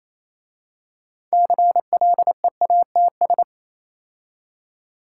Morse code, heard as a keyed tone, about 27 wpm, 690 Hz. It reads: CLEATS
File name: Easy Words - Speed-Racing with Hints - 22wpm